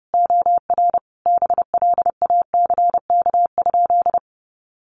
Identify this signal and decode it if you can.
Morse code, heard as a keyed tone, about 30 wpm, 700 Hz. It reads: OR BLACK?
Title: Sentences from Top 400 Words 30wpm